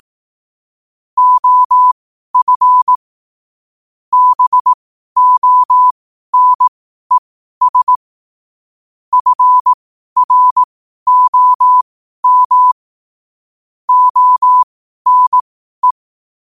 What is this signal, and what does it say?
Morse code, heard as a keyed tone, about 18 wpm, spaced as for 12 wpm, 1000 Hz. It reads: OF BONES FROM ONE
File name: qotdfiles / 12wpm